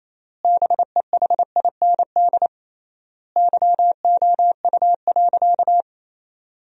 Morse code, heard as a keyed tone, about 28 wpm, 700 Hz. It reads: BEHIND YOU.
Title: 3x Word-Spacing - Sentences from Top 500 Words - 28wpm